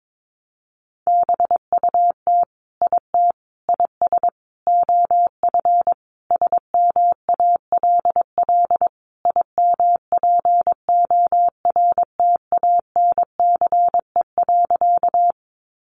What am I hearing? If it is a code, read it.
Morse code, heard as a keyed tone, about 22 wpm, 700 Hz. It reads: BUT IT IS OF SMALL IMPORTANCE.
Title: Sentences from Top 1200 Words 22wpm